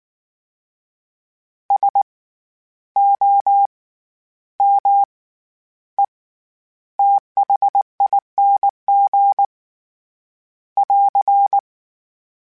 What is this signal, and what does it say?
Morse code, heard as a keyed tone, about 19 wpm, spaced as for 11 wpm, 800 Hz. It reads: SOMETHING <AR>
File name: Special INGs / Special INGs-mp3s / SlowToFaster ing @18